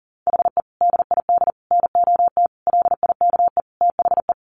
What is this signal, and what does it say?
Morse code, heard as a keyed tone, about 40 wpm, 700 Hz. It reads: HE DID NOT LIKE THE
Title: Sentences from Top 1200 Words 40wpm